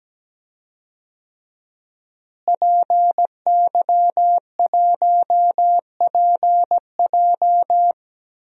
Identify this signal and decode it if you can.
Morse code, heard as a keyed tone, about 17 wpm, 700 Hz. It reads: PY1PJ
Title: DXCC-100 Call Signs - 5 Characters - Repeated 3x - 17wpm